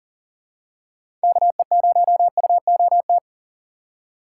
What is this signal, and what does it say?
Morse code, heard as a keyed tone, about 40 wpm, 700 Hz. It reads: KE0UOT
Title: USA Call Signs - 6 Characters - 2x Character Spacing - 40wpm